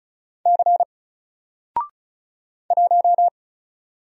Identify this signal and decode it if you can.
Morse code, heard as a keyed tone, about 36 wpm, 700 Hz. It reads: C 1
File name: Single-Letter-Number-35wpm